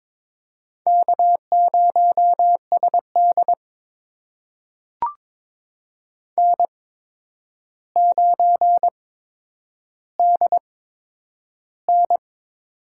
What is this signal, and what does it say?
Morse code, heard as a keyed tone, about 22 wpm, spaced as for 9 wpm, 700 Hz. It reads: K0SD N9DN
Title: USA Call Signs - 4 Characters - 8x Character Spacing - 22wpm